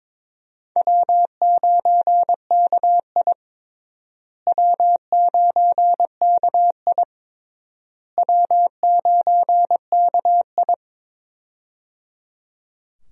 Morse code, heard as a keyed tone, about 22 wpm, 700 Hz. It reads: W9KI W9KI W9KI
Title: USA Call Signs - 4 Characters - Repeated 3x - 22wpm